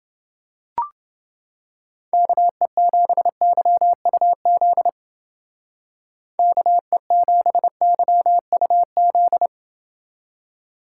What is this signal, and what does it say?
Morse code, heard as a keyed tone, about 29 wpm, 700 Hz. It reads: KE7YUZ KE7YUZ